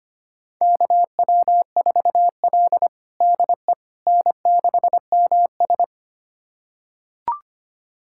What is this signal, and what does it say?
Morse code, heard as a keyed tone, about 25 wpm, 700 Hz. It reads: KW4L DE N6MS
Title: USA Callsign DE Callsign - 4 Characters - 25wpm